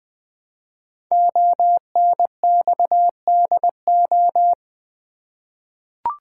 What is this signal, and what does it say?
Morse code, heard as a keyed tone, about 20 wpm, 700 Hz. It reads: ONXDO
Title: Code Groups - 5 Letters - 20wpm